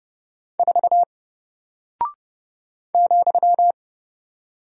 Morse code, heard as a keyed tone, about 30 wpm, 700 Hz. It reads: 4 ,
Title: Single Character (Letter:Number:Punctuation:Prosign) 30wpm